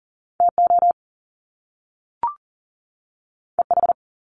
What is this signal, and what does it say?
Morse code, heard as a keyed tone, about 42 wpm, 700 Hz. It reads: TO EH
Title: Top Two Letter Combinations in Words 40wpm